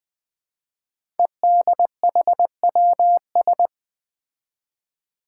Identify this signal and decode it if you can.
Morse code, heard as a keyed tone, about 20 wpm, 700 Hz. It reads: EDHWS
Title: Five Challenging Letter-Number Combinations 20wpm